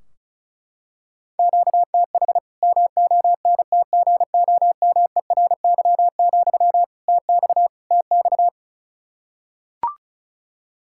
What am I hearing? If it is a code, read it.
Morse code, heard as a keyed tone, about 36 wpm, 700 Hz. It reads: QTH MONTGOMERY, TX TX
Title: K1USN SST QSO Practice 35wpm